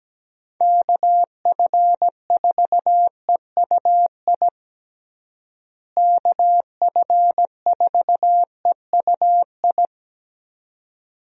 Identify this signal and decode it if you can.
Morse code, heard as a keyed tone, about 17 wpm, 700 Hz. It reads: KF4EUI KF4EUI
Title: USA Call Signs - 6 Characters - Repeated 2x - 17wpm